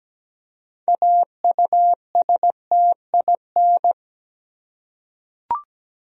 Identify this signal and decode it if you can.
Morse code, heard as a keyed tone, about 17 wpm, 700 Hz. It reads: AUSTIN
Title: USA Common Names - 17wpm